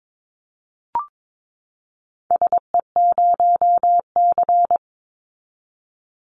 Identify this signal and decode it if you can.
Morse code, heard as a keyed tone, about 22 wpm, 700 Hz. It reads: SE0C
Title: Call Signs Repeated 3x - 22wpm